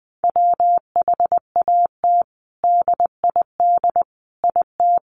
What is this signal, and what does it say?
Morse code, heard as a keyed tone, about 20 wpm, 700 Hz. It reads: WHAT DID IT
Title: Sentences from Top 1100 Words 20wpm